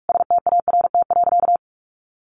Morse code, heard as a keyed tone, about 44 wpm, 700 Hz. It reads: START.
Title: Sentences from Top 300 Words 45wpm